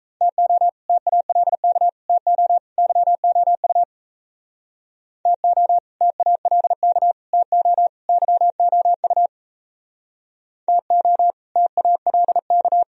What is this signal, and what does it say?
Morse code, heard as a keyed tone, about 37 wpm, 690 Hz. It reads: TO TALK TO YOU TO TALK TO YOU TO TALK